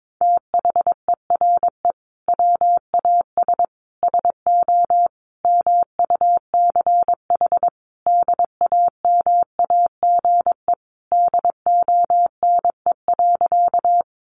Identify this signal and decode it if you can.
Morse code, heard as a keyed tone, about 22 wpm, 700 Hz. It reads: THERE WAS SO MUCH DAMAGE DONE.
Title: Sentences from Top 1100 Words 22wpm